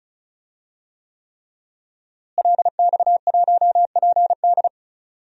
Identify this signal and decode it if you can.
Morse code, heard as a keyed tone, about 36 wpm, 700 Hz. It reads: LX1PD
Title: DXCC-100 Call Signs - 5 Characters - Repeated 3x - 35wpm